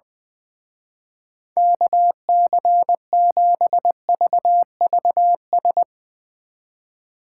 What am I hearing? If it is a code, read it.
Morse code, heard as a keyed tone, about 20 wpm, 700 Hz. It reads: KC7VVS
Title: Speed Racing - USA 2x3 Call Signs 20wpm